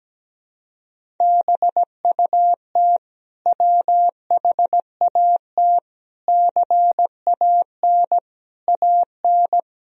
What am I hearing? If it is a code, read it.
Morse code, heard as a keyed tone, about 17 wpm, 700 Hz. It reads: BUT WHAT CAN AN
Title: Sentences from Top 1000 Words - Review Part 4 - 17wpm